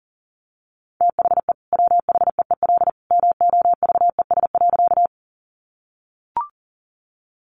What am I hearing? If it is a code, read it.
Morse code, heard as a keyed tone, about 40 wpm, 700 Hz. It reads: THE WHEEL MOVES.
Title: Additional Easy Phrases 40wpm